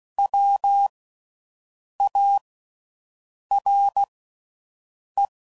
Morse code, heard as a keyed tone, about 16 wpm, 800 Hz. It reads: W A R E
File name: Suffixes / General / Houses